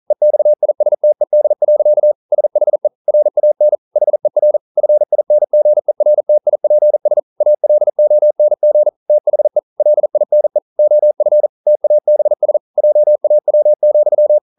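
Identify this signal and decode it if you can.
Morse code, heard as a keyed tone, about 41 wpm, 600 Hz. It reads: EXISTED. SHE RAN HER FINGERTIPS ALONG THE LINE OF TABS JAW,
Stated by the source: Morse code practice at 41 wpm, from CW med Gnister.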